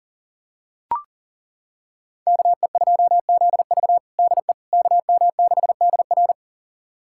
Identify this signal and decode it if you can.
Morse code, heard as a keyed tone, about 40 wpm, 700 Hz. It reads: KE2ZV DE KM6DR
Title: USA Callsign DE Callsign - 5 Characters - 40wpm